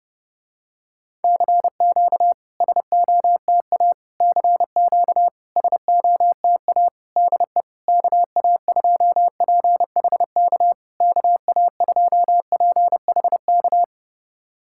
Morse code, heard as a keyed tone, about 31 wpm, 690 Hz. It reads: CQ SOTA CQ SOTA DE KA2PHK KA2PHK
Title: SOTA QSO Practice 30wpm